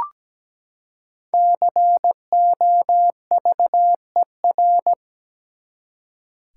Morse code, heard as a keyed tone, about 17 wpm, 700 Hz. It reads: COVER
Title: Five Letter Words - 17wpm